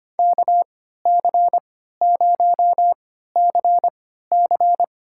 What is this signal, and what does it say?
Morse code, heard as a keyed tone, about 25 wpm, spaced as for 14 wpm, 700 Hz. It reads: KC0CC
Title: USA Call Signs - 5 Characters - 3x Character Spacing - 25wpm